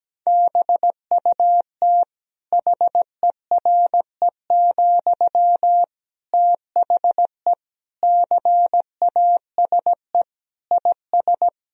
Morse code, heard as a keyed tone, about 17 wpm, 700 Hz. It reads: BUT HERE, THE CASE IS
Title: Sentences from Top 1000 Words - Review Part 3 - 17wpm